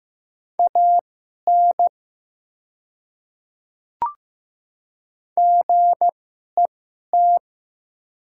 Morse code, heard as a keyed tone, about 15 wpm, spaced as for 11 wpm, 700 Hz. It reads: AN GET
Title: 2x Character Spacing - Top 200 Words - 15wpm